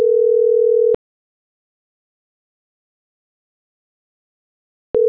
Ringback tone (North America), 440 + 480 Hz.